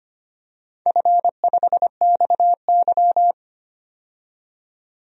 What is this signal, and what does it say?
Morse code, heard as a keyed tone, about 25 wpm, 700 Hz. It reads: F5XY
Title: DXCC-100 Call Signs - 4 Characters - 25wpm